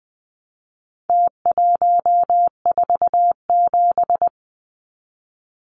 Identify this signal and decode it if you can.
Morse code, heard as a keyed tone, about 20 wpm, 700 Hz. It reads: T147
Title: Numbers - Contest Style - 4-Digit 20wpm